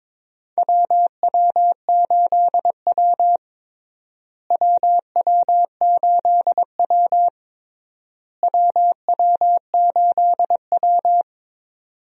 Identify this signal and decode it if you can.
Morse code, heard as a keyed tone, about 22 wpm, 700 Hz. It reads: WW8W WW8W WW8W
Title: USA Call Signs - 4 Characters - Repeated 3x - 22wpm